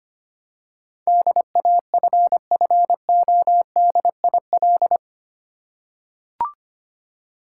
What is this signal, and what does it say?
Morse code, heard as a keyed tone, about 25 wpm, 700 Hz. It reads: DAFFODIL
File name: practice / 25wpm / Difficult Words with Hints - 25wpm